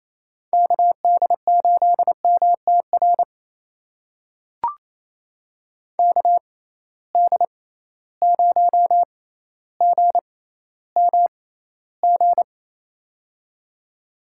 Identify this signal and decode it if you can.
Morse code, heard as a keyed tone, about 27 wpm, spaced as for 14 wpm, 700 Hz. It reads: KD8MTR KD0GMG
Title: USA Call Signs - 6 Characters - 6x Character Spacing - 28wpm